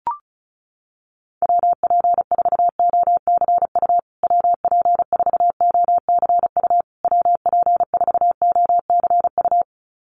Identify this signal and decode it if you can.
Morse code, heard as a keyed tone, about 34 wpm, 700 Hz. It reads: WP4OCU WP4OCU WP4OCU
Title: USA Call Signs Repeated 3x 35wpm